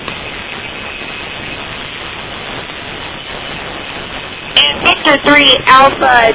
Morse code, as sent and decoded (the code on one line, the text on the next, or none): none